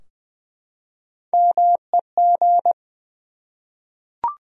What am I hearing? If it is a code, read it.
Morse code, heard as a keyed tone, about 20 wpm, 700 Hz. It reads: MEG
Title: USA Common Names - Max 3 Letters - 20wpm